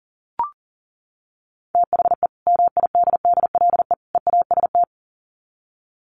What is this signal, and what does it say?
Morse code, heard as a keyed tone, about 40 wpm, 700 Hz. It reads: THE MIDDLE EAST
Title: Sets of 3 Words 40wpm